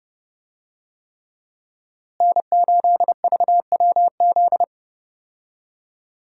Morse code, heard as a keyed tone, about 32 wpm, 700 Hz. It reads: N8VWZ
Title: USA Call Signs - 5 Characters - Repeated 3x - 30wpm